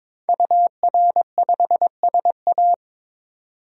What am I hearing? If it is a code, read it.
Morse code, heard as a keyed tone, about 22 wpm, 700 Hz. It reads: UR5SA
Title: DXCC-100 Call Signs - 5 Characters - 22wpm